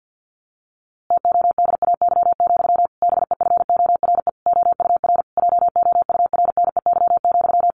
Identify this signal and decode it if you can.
Morse code, heard as a keyed tone, about 50 wpm, 700 Hz. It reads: TODAY, BEFORE OUR JOURNEY,
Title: A Sound Of Thunder 50wpm